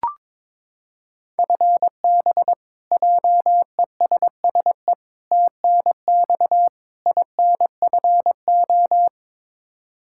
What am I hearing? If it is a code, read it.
Morse code, heard as a keyed tone, about 22 wpm, 700 Hz. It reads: FB JESSE TNX INFO
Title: Practice QSO - Exchange 3 - More Info - 22wpm